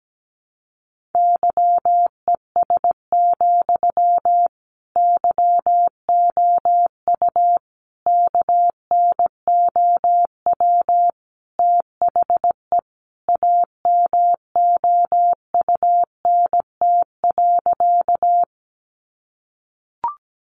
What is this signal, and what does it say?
Morse code, heard as a keyed tone, about 17 wpm, 700 Hz. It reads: YES, YOU KNOW THE AMOUNT.
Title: Sentences from Top 700 Words 17wpm